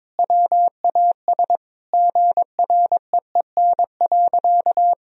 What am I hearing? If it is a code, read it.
Morse code, heard as a keyed tone, about 22 wpm, 700 Hz. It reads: WAS GREEN.
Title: Sentences from Top 1000 Words - Review Part 2 - 22wpm